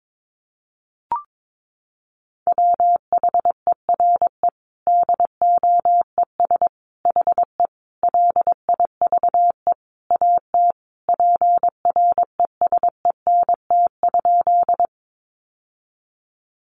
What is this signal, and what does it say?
Morse code, heard as a keyed tone, about 22 wpm, 700 Hz. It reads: WHERE DOES HE LIVE AT PRESENT?
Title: Sentences from Top 800 Words 22wpm